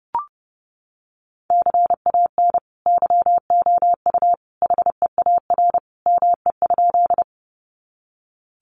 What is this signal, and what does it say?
Morse code, heard as a keyed tone, about 30 wpm, 700 Hz. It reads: CAN YOU HEAR ME?
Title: Sentences from Top 300 Words 30wpm